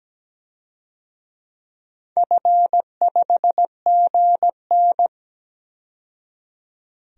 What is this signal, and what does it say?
Morse code, heard as a keyed tone, about 17 wpm, 700 Hz. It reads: F5GN
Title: DXCC-100 Call Signs - 4 Characters - 17wpm